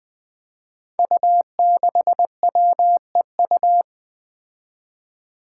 Morse code, heard as a keyed tone, about 20 wpm, 700 Hz. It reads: U6WEU
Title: Five Challenging Letter-Number Combinations 20wpm